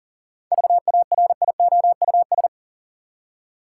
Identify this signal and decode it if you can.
Morse code, heard as a keyed tone, about 40 wpm, 700 Hz. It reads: VARIOUS
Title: Seven Letter Words - 40wpm